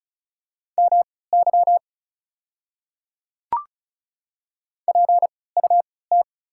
Morse code, heard as a keyed tone, about 35 wpm, spaced as for 20 wpm, 690 Hz. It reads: MY PUT
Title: 3x Character Spacing - Top 300 Words - 35wpm